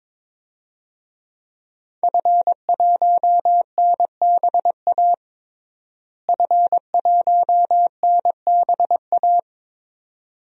Morse code, heard as a keyed tone, about 23 wpm, 700 Hz. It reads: F1NBA F1NBA
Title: DXCC-100 Call Signs - 5 Characters - Repeated 2x - 22wpm